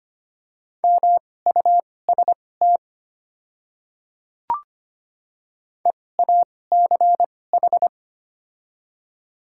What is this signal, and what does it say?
Morse code, heard as a keyed tone, about 25 wpm, spaced as for 18 wpm, 690 Hz. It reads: MUST EACH
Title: 2x Character Spacing - Top 200 Words - 25wpm